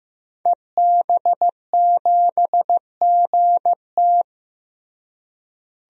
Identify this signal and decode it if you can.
Morse code, heard as a keyed tone, about 16 wpm, 700 Hz. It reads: EB7GT
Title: DXCC-100 Call Signs - 5 Characters - Repeated 2x - 15wpm